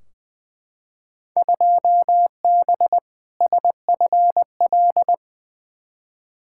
Morse code, heard as a keyed tone, about 21 wpm, 700 Hz. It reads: 2B SFL
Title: Field Day Exchanges - 20wpm